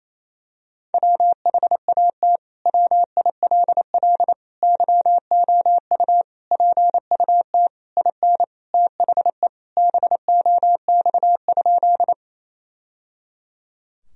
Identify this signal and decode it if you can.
Morse code, heard as a keyed tone, about 28 wpm, 700 Hz. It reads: WHAT WILL YOU PUT IN THE BOX?
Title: Sentences from Top 1000 Words - Review Part 2 - 28wpm